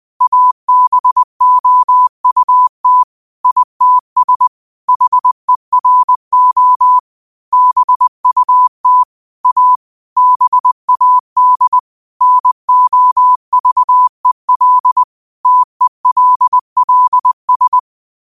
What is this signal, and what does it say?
Morse code, heard as a keyed tone, about 20 wpm, 1000 Hz. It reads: ABOUT ITS HERO BUT A BAD NOVEL TELLS